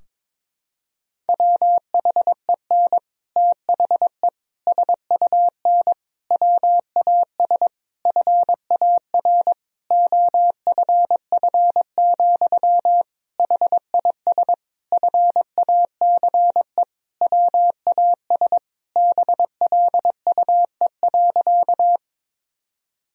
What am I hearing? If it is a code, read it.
Morse code, heard as a keyed tone, about 23 wpm, 700 Hz. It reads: WHEN THE SUN WAS FAR OFF, HIS FACE WAS BLUE.
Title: Sentences from Top 1000 Words - Review Part 3 - 22wpm